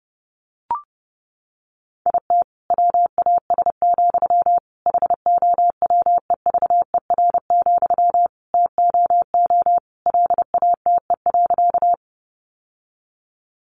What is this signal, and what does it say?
Morse code, heard as a keyed tone, about 30 wpm, 700 Hz. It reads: IT WAS, HOWEVER, TOO LATE.
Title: Sentences from Top 600 Words 30wpm